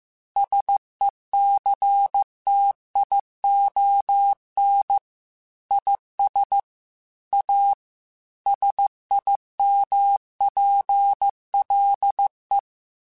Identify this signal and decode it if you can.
Morse code, heard as a keyed tone, about 15 wpm, 800 Hz. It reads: SECTION IS A SIMPLE
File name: Technical Subjects / Technical Subjects-mp3s / Sw Pwr Sup @13